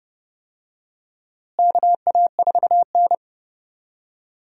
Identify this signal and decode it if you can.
Morse code, heard as a keyed tone, about 30 wpm, 700 Hz. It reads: KA4N
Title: USA Call Signs - 4 Characters - Repeated 2x - 30wpm